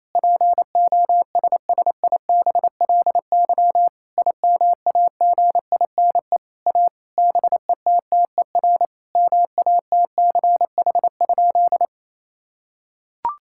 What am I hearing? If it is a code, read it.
Morse code, heard as a keyed tone, about 28 wpm, 700 Hz. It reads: POSSIBLY IMAGINE A BETTER MATCH?
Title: Sentences from Top 1600 Words 28wpm